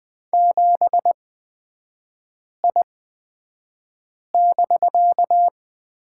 Morse code, heard as a keyed tone, about 20 wpm, spaced as for 5 wpm, 700 Hz. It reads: 7I<BK>
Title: 250 Lesson 40e - BK - Three Characters 20wpm